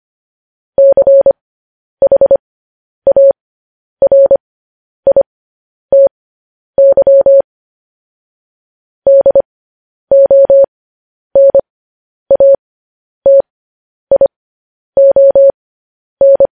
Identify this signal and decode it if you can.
Morse code, heard as a keyed tone, about 25 wpm, spaced as for 10 wpm, 550 Hz. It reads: CHARITY DONATION
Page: news headlines in morse code at 10 wpm